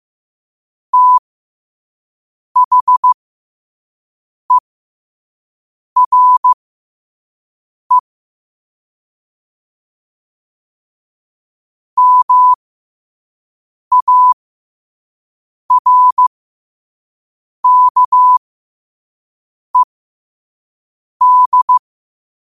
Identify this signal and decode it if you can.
Morse code, heard as a keyed tone, about 15 wpm, spaced as for 5 wpm, 1000 Hz. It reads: THERE MARKED
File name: qotdfiles / 5wpm